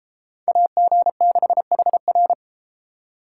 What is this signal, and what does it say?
Morse code, heard as a keyed tone, about 32 wpm, 700 Hz. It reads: AG6HR